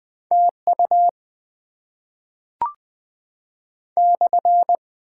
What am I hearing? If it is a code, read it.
Morse code, heard as a keyed tone, about 20 wpm, 700 Hz. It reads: TU /
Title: Top 90 QSO Elements - 6x Character Spacing - 20wpm